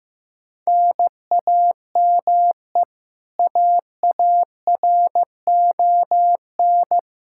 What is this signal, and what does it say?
Morse code, heard as a keyed tone, about 15 wpm, 700 Hz. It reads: NAME AARON
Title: Practice QSO - Exchange 1 - Beginning - 15wpm